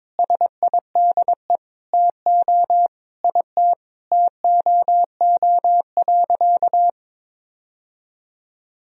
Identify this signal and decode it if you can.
Morse code, heard as a keyed tone, about 22 wpm, 700 Hz. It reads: SIDE TO IT TOO.
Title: Sentences from Top 1000 Words - Review Part 1 - 22wpm